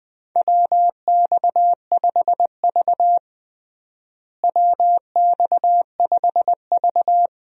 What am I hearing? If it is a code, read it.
Morse code, heard as a keyed tone, about 20 wpm, 700 Hz. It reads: WX5V WX5V
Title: USA Call Signs - 4 Characters - Repeated 2x - 20wpm